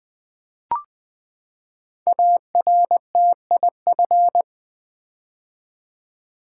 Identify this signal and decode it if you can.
Morse code, heard as a keyed tone, about 20 wpm, 700 Hz. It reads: ARTIF
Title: Build A Word - 5 Letters At A Time - 10-Letter Words - 20wpm